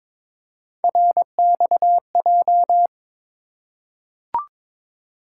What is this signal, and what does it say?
Morse code, heard as a keyed tone, about 22 wpm, 700 Hz. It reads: RXJ
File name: practice / 22wpm / Three Letter-Number Combinations - 22wpm